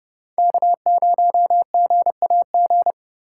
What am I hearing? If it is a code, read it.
Morse code, heard as a keyed tone, about 39 wpm, 700 Hz. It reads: K0GAG